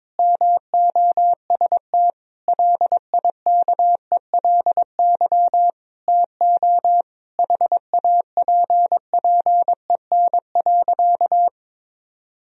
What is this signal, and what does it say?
Morse code, heard as a keyed tone, about 22 wpm, 700 Hz. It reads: MOST LIKELY TO HAPPEN.
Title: Sentences from Top 1000 Words - Review Part 4 - 22wpm